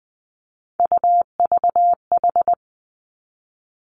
Morse code, heard as a keyed tone, about 20 wpm, 700 Hz. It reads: UVH
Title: Three Challenging Letter-Number Combinations 20wpm